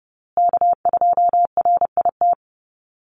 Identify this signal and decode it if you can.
Morse code, heard as a keyed tone, about 30 wpm, 700 Hz. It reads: K2RIT
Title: USA Call Signs Repeated 2x 30wpm